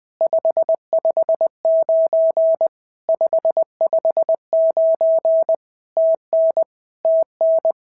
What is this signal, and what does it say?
Morse code, heard as a keyed tone, about 20 wpm, 650 Hz. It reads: 559 559 TN TN
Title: POTA QSO Practice - 20wpm